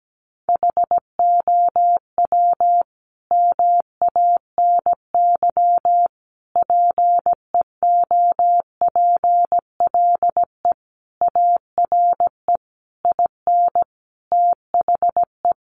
Morse code, heard as a keyed tone, about 17 wpm, 700 Hz. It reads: HOW MANY PEOPLE ARE IN THE
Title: Sentences from Top 1200 Words 17wpm